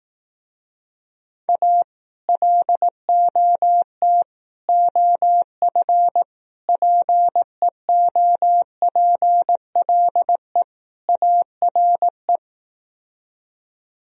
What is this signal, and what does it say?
Morse code, heard as a keyed tone, about 18 wpm, 700 Hz. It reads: A LOT OF PEOPLE ARE
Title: Speed-Racing - Sets of 5 Words 15wpm